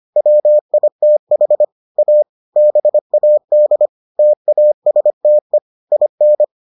Morse code, heard as a keyed tone, about 25 wpm, 600 Hz. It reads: WITH A BAD TASTE IN